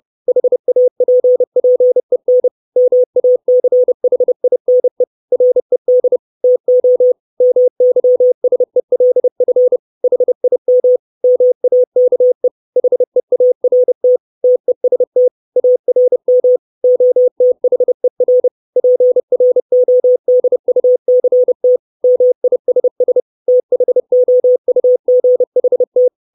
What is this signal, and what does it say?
Morse code, heard as a keyed tone, about 30 wpm, 500 Hz. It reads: HAPPEN MACHINE RED TO MYSELF HIM MAKE HEART TEST ARM OTHER PRODUCT MISS THOUGHT
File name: 632 Words 30WPM